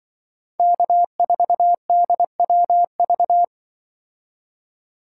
Morse code, heard as a keyed tone, about 24 wpm, 700 Hz. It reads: K4DWV
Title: Speed Racing - USA 1x3 Call Signs 20wpm